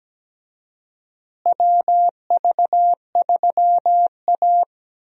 Morse code, heard as a keyed tone, about 17 wpm, 700 Hz. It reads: WV3A
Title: USA Call Signs - 4 Characters - Repeated 3x - 17wpm